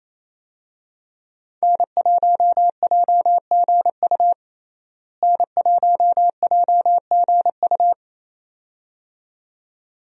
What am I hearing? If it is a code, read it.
Morse code, heard as a keyed tone, about 28 wpm, 700 Hz. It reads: N1JGU N1JGU
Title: USA Call Signs - 5 Characters - Repeated 2x - 28wpm